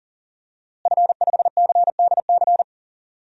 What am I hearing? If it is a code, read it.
Morse code, heard as a keyed tone, about 37 wpm, 700 Hz. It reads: F5CDC
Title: DXCC-100 Call Signs - 5 Characters - Repeated 2x - 40wpm